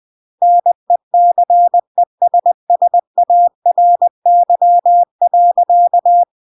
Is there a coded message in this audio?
Yes: Morse code from 0.4 to 6.2 s, heard as a keyed tone, about 20 wpm, 700 Hz. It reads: NECESSARY.